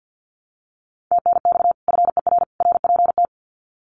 Morse code, heard as a keyed tone, about 48 wpm, 700 Hz. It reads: TNX FER RPT
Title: Common QSO Phrases 50wpm